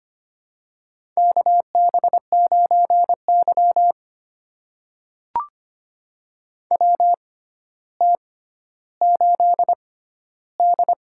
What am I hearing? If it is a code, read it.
Morse code, heard as a keyed tone, about 25 wpm, spaced as for 13 wpm, 700 Hz. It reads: KB9Y WT8D
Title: USA Call Signs - 4 Characters - 6x Character Spacing - 25wpm